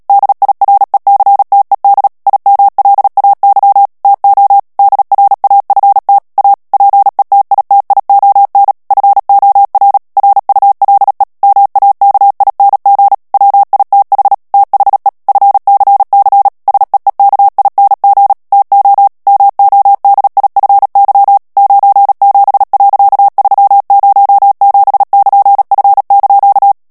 Morse code, heard as a keyed tone, about 37 wpm, 800 Hz. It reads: DIRECTED IMLAY TO DRAFT A PETITION FOR RULE MAKING WITH THE FCC SEEKING TO MODIFY 97.307<KN>F)
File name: morse mp3 / ANGLAIS texte clair 3 vitesse 35 mots minute